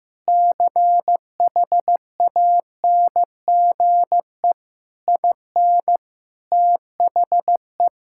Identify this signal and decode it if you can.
Morse code, heard as a keyed tone, about 15 wpm, 700 Hz. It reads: CHANGE IN THE
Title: Sentences from Top 1000 Words - Review Part 6 - 15wpm